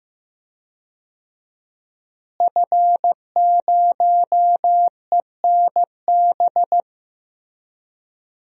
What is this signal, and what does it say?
Morse code, heard as a keyed tone, about 15 wpm, 700 Hz. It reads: F0ENB
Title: DXCC-100 Call Signs - 5 Characters - Repeated 3x - 15wpm